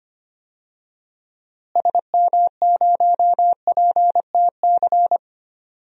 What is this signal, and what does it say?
Morse code, heard as a keyed tone, about 25 wpm, 700 Hz. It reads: SM0PTC
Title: DXCC-100 Call Signs - 6 Characters - Repeated 3x - 25wpm